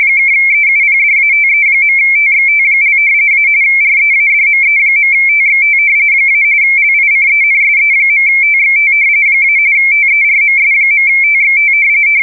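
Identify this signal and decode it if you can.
RTTY, about 45.45 baud. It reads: (EUPDATE.TXT) - README.TXT - LICENSE.TXT - TERMS.TXT
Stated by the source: RTTY test recording for the JE3HHT MMTTY engine used with UcxLog logging software.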